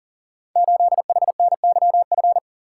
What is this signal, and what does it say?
Morse code, heard as a keyed tone, about 42 wpm, 700 Hz. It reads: 8HNYF
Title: Code Groups - 5 Letter:Numbers - 40wpm